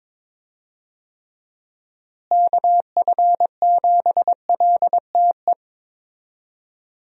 Morse code, heard as a keyed tone, about 21 wpm, 700 Hz. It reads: KF7LTE
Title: USA Call Signs - 6 Characters - Repeated 2x - 22wpm